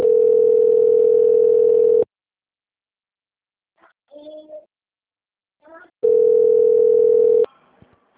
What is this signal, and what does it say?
Ringback tone (North America), 440 + 480 Hz.